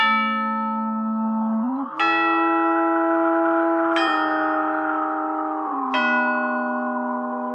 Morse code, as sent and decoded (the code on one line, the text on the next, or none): none